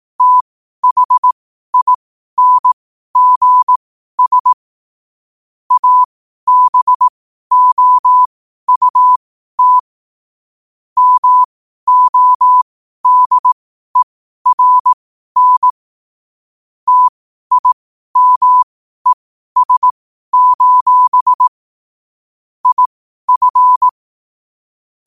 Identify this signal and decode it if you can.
Morse code, heard as a keyed tone, about 18 wpm, spaced as for 12 wpm, 1000 Hz. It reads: THINGS ABOUT MODERN TIMES: IF
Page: qotd-at-12-wpm